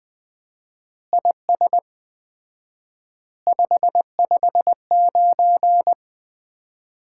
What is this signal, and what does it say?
Morse code, heard as a keyed tone, about 21 wpm, 700 Hz. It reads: IS 559
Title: Common QSO Phrases - 4x Word-Spacing - 20wpm